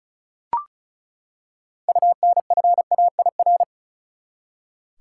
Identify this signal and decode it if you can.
Morse code, heard as a keyed tone, about 36 wpm, 700 Hz. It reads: UNFAIR
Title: Six Letter Words - 35wpm